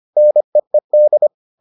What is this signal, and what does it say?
Morse code, heard as a keyed tone, about 25 wpm, 600 Hz. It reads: NEED